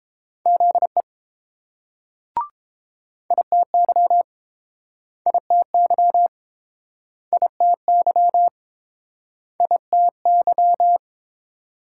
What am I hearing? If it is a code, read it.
Morse code, heard as a keyed tone, about 30 wpm, 700 Hz. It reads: ZE ITY ITY ITY ITY